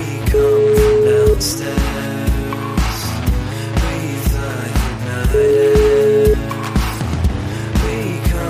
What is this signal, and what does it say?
Ringback tone (Europe, 425 Hz), with noise.